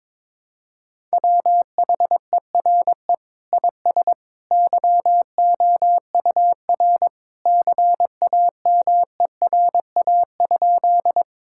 Morse code, heard as a keyed tone, about 22 wpm, 700 Hz. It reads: WHERE IS YOUR CAMERA?
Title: Sentences from Top 1000 Words - Review Part 5 - 22wpm